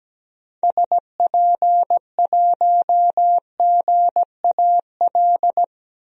Morse code, heard as a keyed tone, about 17 wpm, 700 Hz. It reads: SP1GAL
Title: DXCC-100 Call Signs - 6 Characters - 17wpm